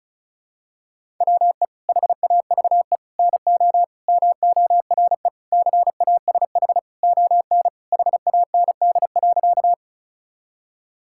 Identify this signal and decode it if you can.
Morse code, heard as a keyed tone, about 36 wpm, 700 Hz. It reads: WE HAVE NO MORE CASH ON HAND.
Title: Sentences from Top 1000 Words - Review Part 5 - 35wpm